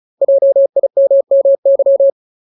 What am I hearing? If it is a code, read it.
Morse code, heard as a keyed tone, about 36 wpm, 550 Hz. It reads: JIMMY
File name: Ultimate Word List / 35WPM Ultimate